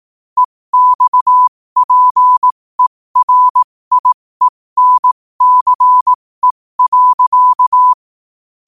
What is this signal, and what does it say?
Morse code, heard as a keyed tone, about 18 wpm, spaced as for 15 wpm, 1000 Hz. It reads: EXPERIENCE.
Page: New quotes every day in morse code at 15 Words per minute.